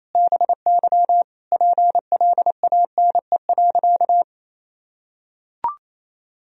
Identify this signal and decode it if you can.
Morse code, heard as a keyed tone, about 28 wpm, 700 Hz. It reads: BY PLANE.
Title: Sentences from Top 1000 Words - Review Part 3 - 28wpm